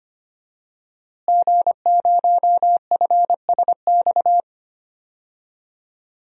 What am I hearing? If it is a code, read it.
Morse code, heard as a keyed tone, about 25 wpm, 700 Hz. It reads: G0FSX
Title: DXCC-100 Call Signs - 5 Characters - Repeated 3x - 25wpm